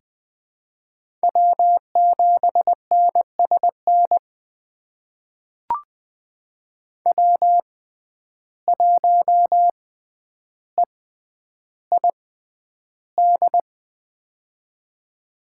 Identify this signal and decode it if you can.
Morse code, heard as a keyed tone, about 20 wpm, spaced as for 10 wpm, 700 Hz. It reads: W7NSN W1EID
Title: USA Call Signs - 5 Characters - 6x Character Spacing - 20wpm